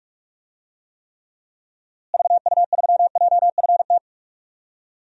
Morse code, heard as a keyed tone, about 41 wpm, 700 Hz. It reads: VU3JFT
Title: DXCC-100 Call Signs - 6 Characters - Repeated 3x - 45wpm